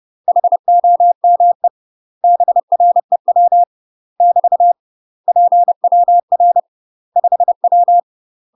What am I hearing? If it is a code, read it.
Morse code, heard as a keyed tone, about 30 wpm, 700 Hz. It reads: HOME BREW <BT> PWR 5W
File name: UK MASTER QSOs TRAINING / 30WPM / UK MASTER QSOs 0002